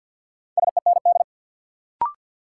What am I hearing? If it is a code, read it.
Morse code, heard as a keyed tone, about 47 wpm, 700 Hz. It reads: SEND
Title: Sentences from Top 1500 Words - New Word Drill Alt - 50wpm